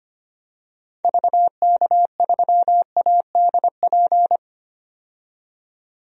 Morse code, heard as a keyed tone, about 25 wpm, 700 Hz. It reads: VK3ADP
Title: DXCC-100 Call Signs - 6 Characters - 25wpm